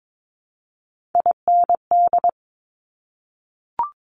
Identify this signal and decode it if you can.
Morse code, heard as a keyed tone, about 22 wpm, 700 Hz. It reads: IND
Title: Top Three Letter Combinations in Words 22wpm